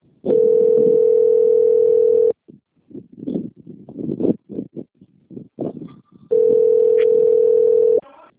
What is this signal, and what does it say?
Ringback tone (North America), 440 + 480 Hz.